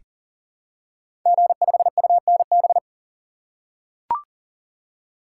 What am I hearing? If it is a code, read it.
Morse code, heard as a keyed tone, about 40 wpm, 700 Hz. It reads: G5UNB
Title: Code Groups - 5 Letter:Numbers - 40wpm